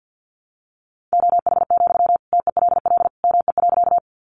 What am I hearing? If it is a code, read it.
Morse code, heard as a keyed tone, about 50 wpm, 700 Hz. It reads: OH, TELL ME.
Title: Sentences from Top 500 Words 50wpm